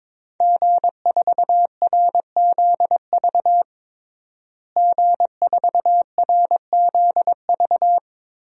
Morse code, heard as a keyed tone, about 22 wpm, 700 Hz. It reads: G4RZV G4RZV
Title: DXCC-100 Call Signs - 5 Characters - Repeated 3x - 22wpm